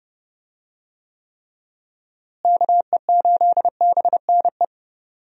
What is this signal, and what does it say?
Morse code, heard as a keyed tone, about 31 wpm, 700 Hz. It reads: KE8BNE
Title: USA Call Signs - 6 Characters - Repeated 2x - 30wpm